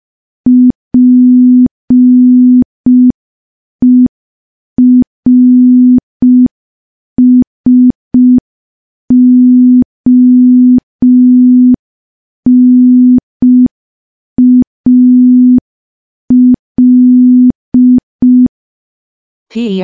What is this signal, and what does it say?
Morse code, heard as a keyed tone, about 5 wpm, 250 Hz. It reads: PERSONAL
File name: morse / morse-training-05WPM